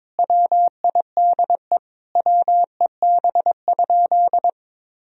Morse code, heard as a keyed tone, about 22 wpm, 700 Hz. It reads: WIDE WEB?
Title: Sentences from Top 1000 Words - Review Part 4 - 22wpm